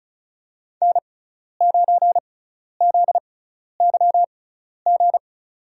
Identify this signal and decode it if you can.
Morse code, heard as a keyed tone, about 36 wpm, spaced as for 12 wpm, 700 Hz. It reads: N9ZYG
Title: USA Call Signs - 5 Characters - 6x Character Spacing - 35wpm